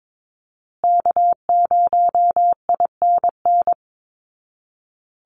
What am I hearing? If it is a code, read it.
Morse code, heard as a keyed tone, about 22 wpm, 700 Hz. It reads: K0INN
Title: Call Signs Repeated 2x - 22wpm